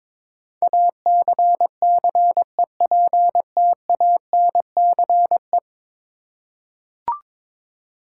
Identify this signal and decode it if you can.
Morse code, heard as a keyed tone, about 22 wpm, 700 Hz. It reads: ACCEPTANCE
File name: practice / 22wpm / 10 Letter Words - First 3 Letters Given - 22wpm